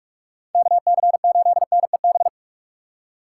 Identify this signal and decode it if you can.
Morse code, heard as a keyed tone, about 44 wpm, 700 Hz. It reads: KC8NEB